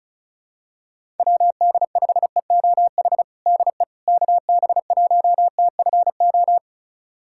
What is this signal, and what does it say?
Morse code, heard as a keyed tone, about 34 wpm, 700 Hz. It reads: WD5EOH DE KB1TFO
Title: USA Callsign DE Callsign - 6 Characters - 35wpm